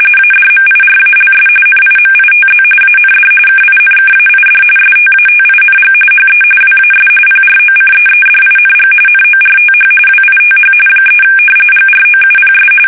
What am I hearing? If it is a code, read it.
RTTY, about 45.45 baud. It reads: ADMIRALS NIMITZ AND KINKAID STATING HE WAS PROCEEDING NORTH WITH 3 GROUPS
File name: fsk850 / battleshipnj